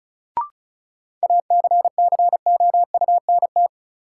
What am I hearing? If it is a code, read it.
Morse code, heard as a keyed tone, about 34 wpm, 700 Hz. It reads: ACCOUNT
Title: Sentences from Top 900 Words - New Word Drill - 35wpm